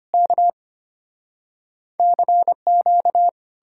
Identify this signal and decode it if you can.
Morse code, heard as a keyed tone, about 33 wpm, 700 Hz. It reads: K CQ